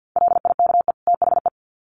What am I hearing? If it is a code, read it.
Morse code, heard as a keyed tone, about 50 wpm, 700 Hz. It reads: LIKE THE
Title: Sets of 2 Words 50wpm